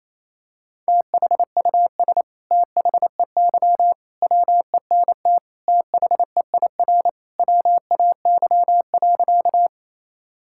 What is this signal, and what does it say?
Morse code, heard as a keyed tone, about 28 wpm, 700 Hz. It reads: THUS THEY WENT THEIR WAY.
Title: Sentences from Top 1000 Words - Review Part 5 - 28wpm